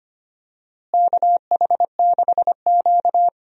Morse code, heard as a keyed tone, about 25 wpm, 700 Hz. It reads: KH6Q
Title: DXCC-100 Call Signs - 4 Characters - Repeated 3x - 25wpm